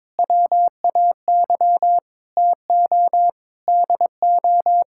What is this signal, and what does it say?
Morse code, heard as a keyed tone, about 22 wpm, 700 Hz. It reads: WAY TO DO
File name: practice / 22wpm / Sentences with 3-Word Phrases - 22wpm